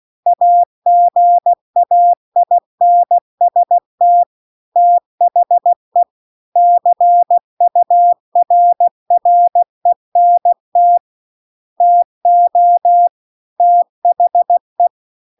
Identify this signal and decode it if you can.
Morse code, heard as a keyed tone, about 16 wpm, 700 Hz. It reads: AGAINST THE CURRENT TO THE
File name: CW-The-Big-Twoheaded-River-2-16WPM